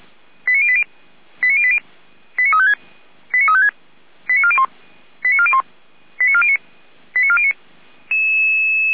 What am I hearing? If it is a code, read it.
Five-tone selective call (ZVEI), heard as quick runs of five tones, tones of about 70 ms. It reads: 89080 89080 89356 89356 89401 89401 89409 89409
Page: /455/SDR/ZVEI/